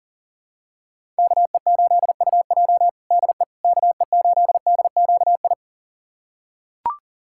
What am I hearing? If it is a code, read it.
Morse code, heard as a keyed tone, about 40 wpm, 700 Hz. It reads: KE8UJ DE KE8DQI
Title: USA Callsign DE Callsign 40wpm